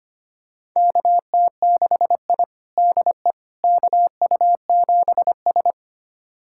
Morse code, heard as a keyed tone, about 25 wpm, 700 Hz. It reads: KT6I DE KU7S
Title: USA Callsign DE Callsign - 4 Characters - 25wpm